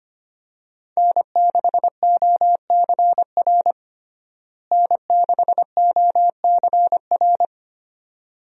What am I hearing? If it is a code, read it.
Morse code, heard as a keyed tone, about 25 wpm, 690 Hz. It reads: N6OCR N6OCR
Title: USA Call Signs - 5 Characters - Repeated 2x - 25wpm